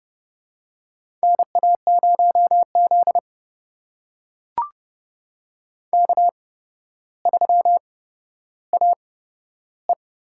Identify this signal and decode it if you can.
Morse code, heard as a keyed tone, about 32 wpm, spaced as for 13 wpm, 700 Hz. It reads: NA0Z K3AE
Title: USA Call Signs - 4 Characters - 8x Character Spacing - 30wpm